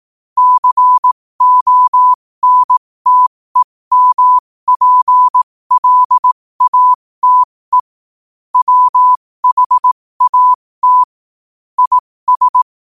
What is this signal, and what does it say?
Morse code, heard as a keyed tone, about 18 wpm, spaced as for 15 wpm, 1000 Hz. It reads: CONTEMPLATE WHAT IS
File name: qotdfiles / 15wpm